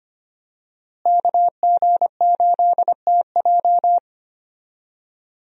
Morse code, heard as a keyed tone, about 25 wpm, 700 Hz. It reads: KG8TJ
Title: USA Call Signs - 5 Characters - Repeated 2x - 25wpm